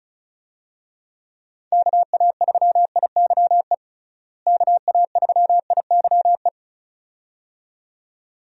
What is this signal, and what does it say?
Morse code, heard as a keyed tone, about 36 wpm, 700 Hz. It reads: KA3IYE KA3IYE
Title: USA Call Signs - 6 Characters - Repeated 2x - 35wpm